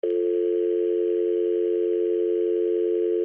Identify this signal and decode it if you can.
Dial tone (North America), 350 + 440 Hz, with noise.
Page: Office Telephone Dial Tone Speakerphone